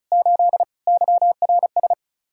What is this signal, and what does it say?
Morse code, heard as a keyed tone, about 44 wpm, 700 Hz. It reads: 8 YRS